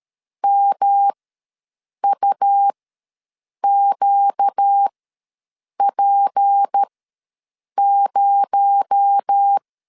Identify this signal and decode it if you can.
Morse code, heard as a keyed tone, about 13 wpm, spaced as for 7 wpm, 800 Hz. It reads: MUQP0
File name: LEZIONI / 6 PKX0 / 6 PKX0 ws 12wpm 3448